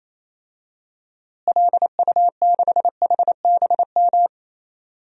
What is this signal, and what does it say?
Morse code, heard as a keyed tone, about 28 wpm, 700 Hz. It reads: LU6HBM
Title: DXCC-100 Call Signs - 6 Characters - Repeated 2x - 28wpm